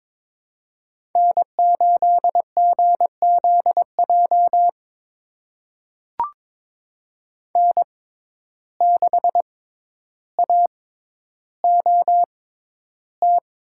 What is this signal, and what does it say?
Morse code, heard as a keyed tone, about 22 wpm, spaced as for 11 wpm, 700 Hz. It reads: N8GZJ N6AOT
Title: USA Call Signs - 5 Characters - 6x Character Spacing - 22wpm